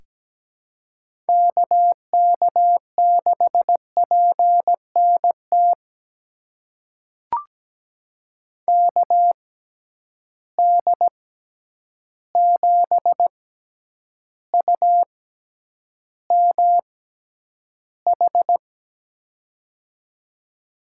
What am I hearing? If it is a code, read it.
Morse code, heard as a keyed tone, about 17 wpm, spaced as for 9 wpm, 700 Hz. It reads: KK6PNT KD7UMH
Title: USA Call Signs - 6 Characters - 6x Character Spacing - 17wpm